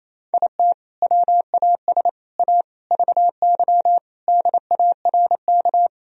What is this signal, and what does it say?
Morse code, heard as a keyed tone, about 28 wpm, 700 Hz. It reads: IT WAS A VY DARK
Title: Sentences from Top 500 Words and Abbreviations 28wpm